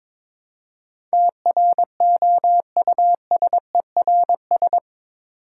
Morse code, heard as a keyed tone, about 22 wpm, 700 Hz. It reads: TROUSERS
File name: practice / 22wpm / Eight Letter Words - 22wpm